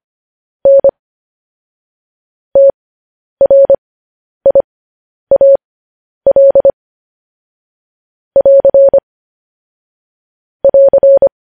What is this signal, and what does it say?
Morse code, heard as a keyed tone, about 25 wpm, spaced as for 10 wpm, 550 Hz. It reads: N TRIAL <AR> <AR>
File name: podcast / News headlines-1x-25x10